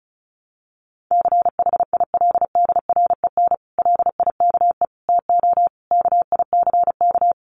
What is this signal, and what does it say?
Morse code, heard as a keyed tone, about 34 wpm, 700 Hz. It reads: CHILDREN LIKE TO KICK
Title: Sentences from Top 1300 Words 35wpm